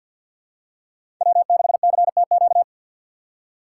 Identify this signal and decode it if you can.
Morse code, heard as a keyed tone, about 47 wpm, 700 Hz. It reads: W6CTQ
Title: USA Call Signs - 5 Characters - 2x Character Spacing - 50wpm